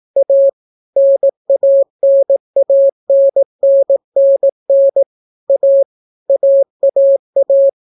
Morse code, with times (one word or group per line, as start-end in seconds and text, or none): 0.2-0.5 A
1.0-5.0 NANANNNN
5.5-5.8 A
6.3-7.7 AAA